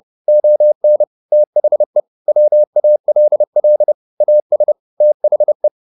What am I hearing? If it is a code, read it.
Morse code, heard as a keyed tone, about 30 wpm, 600 Hz. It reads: ON THE WALL AS THE